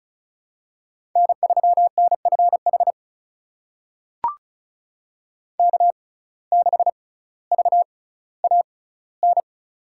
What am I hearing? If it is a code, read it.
Morse code, heard as a keyed tone, about 36 wpm, spaced as for 18 wpm, 700 Hz. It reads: N3NFH K6VAN
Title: USA Call Signs - 5 Characters - 6x Character Spacing - 35wpm